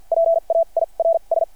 Morse code, heard as a keyed tone, about 52 wpm, spaced as for 38 wpm, 660 Hz. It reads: PAIRS
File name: archived / CRSnet